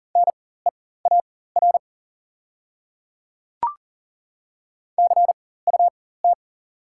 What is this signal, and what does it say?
Morse code, heard as a keyed tone, about 38 wpm, spaced as for 19 wpm, 690 Hz. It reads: NEAR CUT
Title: 4x Character Spacing - Top 300 Words - 40wpm